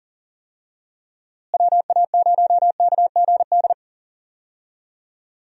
Morse code, heard as a keyed tone, about 40 wpm, 700 Hz. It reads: WA0KGD
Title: USA Call Signs - 6 Characters - Repeated 2x - 40wpm